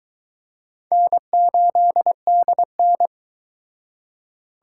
Morse code, heard as a keyed tone, about 23 wpm, 700 Hz. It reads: N8DN